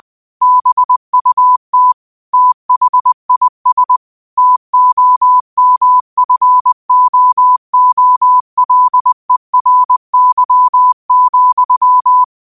Morse code, heard as a keyed tone, about 20 wpm, 1000 Hz. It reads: BUT THIS TOMFOOLERY,
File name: qotdfiles / 20wpm